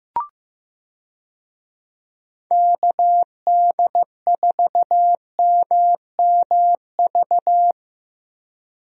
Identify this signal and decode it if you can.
Morse code, heard as a keyed tone, about 15 wpm, 700 Hz. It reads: KD4MMV
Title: USA Call Signs - 6 Characters - Repeated 3x - 15wpm